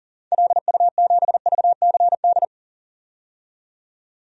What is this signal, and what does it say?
Morse code, heard as a keyed tone, about 38 wpm, 700 Hz. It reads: LU7VCD
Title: DXCC-100 Call Signs - 6 Characters - 40wpm